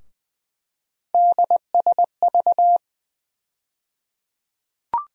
Morse code, heard as a keyed tone, about 20 wpm, 700 Hz. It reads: DSV
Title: USA Call Sign Postfixes - 3 Letters 20wpm